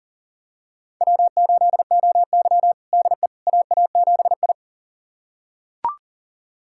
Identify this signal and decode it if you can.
Morse code, heard as a keyed tone, about 40 wpm, 700 Hz. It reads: W8OY DE AA7I
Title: USA Callsign DE Callsign - 4 Characters - 40wpm